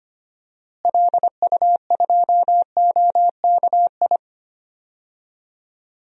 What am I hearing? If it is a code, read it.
Morse code, heard as a keyed tone, about 25 wpm, 690 Hz. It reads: LU2OKI
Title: DXCC-100 Call Signs - 6 Characters - Repeated 3x - 25wpm